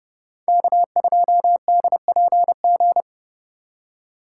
Morse code, heard as a keyed tone, about 34 wpm, 700 Hz. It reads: K2DPG
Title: Speed Racing - USA Call Signs 30wpm